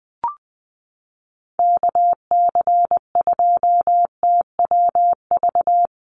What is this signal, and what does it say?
Morse code, heard as a keyed tone, about 20 wpm, 700 Hz. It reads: KC2TWV
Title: USA Call Signs Repeated 3x 20wpm